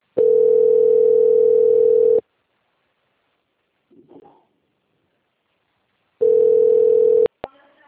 Ringback tone (North America), 440 + 480 Hz.